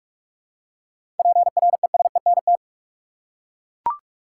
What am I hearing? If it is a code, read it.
Morse code, heard as a keyed tone, about 42 wpm, 700 Hz. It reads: PRESENT
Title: Seven Letter Words - 45wpm